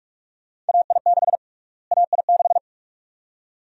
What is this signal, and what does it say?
Morse code, heard as a keyed tone, about 46 wpm, 690 Hz. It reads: AI6 AI6
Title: USA Call Sign Prefixes - Repeated 2x - 45wpm